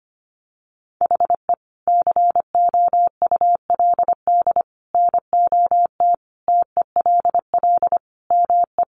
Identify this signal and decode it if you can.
Morse code, heard as a keyed tone, about 25 wpm, 700 Hz. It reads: HE COULD NOT TELL ME
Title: Sentences from Top 600 Words 25wpm